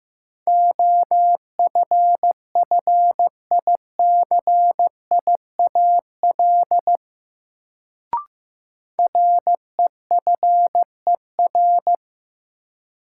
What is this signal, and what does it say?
Morse code, heard as a keyed tone, about 15 wpm, 700 Hz. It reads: OFFICIAL REFER
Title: Sentences from Top 1000 Words - New Word Drill - 15wpm